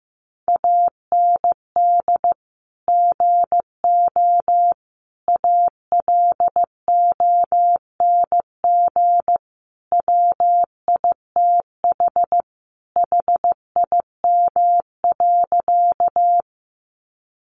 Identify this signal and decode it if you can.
Morse code, heard as a keyed tone, about 15 wpm, 700 Hz. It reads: AND GO ALONG WITH HIM.
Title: Sentences from Top 700 Words 15wpm